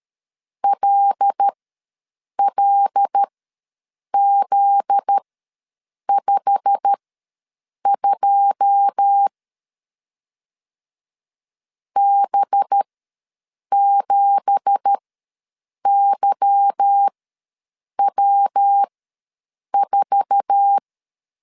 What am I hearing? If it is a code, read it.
Morse code, heard as a keyed tone, about 13 wpm, spaced as for 7 wpm, 800 Hz. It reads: LLZ52 B7YW4
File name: LEZIONI / 9 2867 ws 12wpm 3431